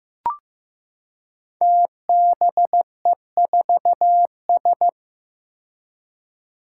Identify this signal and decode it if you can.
Morse code, heard as a keyed tone, about 15 wpm, 700 Hz. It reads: TBE4S
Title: Five Challenging Letter-Number Combinations 15wpm